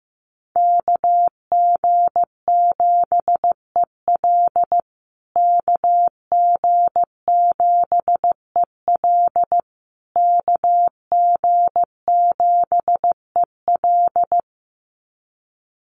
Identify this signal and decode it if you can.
Morse code, heard as a keyed tone, about 15 wpm, 700 Hz. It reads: KG7EL KG7EL KG7EL
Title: USA Call Signs Repeated 3x 15wpm